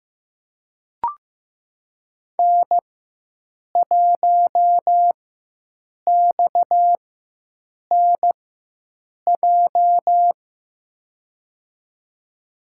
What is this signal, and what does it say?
Morse code, heard as a keyed tone, about 15 wpm, spaced as for 7 wpm, 700 Hz. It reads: N1XNJ
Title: USA Call Signs - 4x Character Spacing - 15wpm